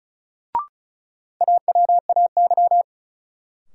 Morse code, heard as a keyed tone, about 34 wpm, 700 Hz. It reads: AWAY
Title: Sentences from Top 600 Words - New Word Drill - 35wpm